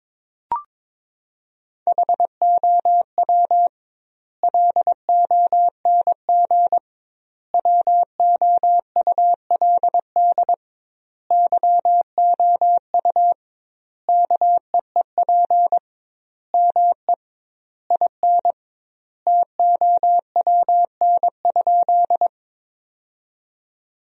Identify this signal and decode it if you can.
Morse code, heard as a keyed tone, about 22 wpm, 700 Hz. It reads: HOW LONG WOULD YOU KEEP ME IN TOWN?
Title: 2x Word-Spacing - Sentences from Top 500 Words - 22wpm